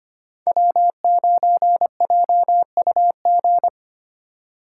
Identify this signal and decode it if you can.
Morse code, heard as a keyed tone, about 25 wpm, 700 Hz. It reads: W9JUG
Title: K1USN SST QSO Practice - 25wpm